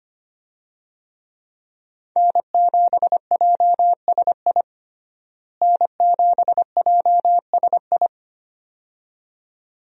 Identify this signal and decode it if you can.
Morse code, heard as a keyed tone, about 25 wpm, 690 Hz. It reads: N7JSI N7JSI
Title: USA Call Signs - 5 Characters - Repeated 2x - 25wpm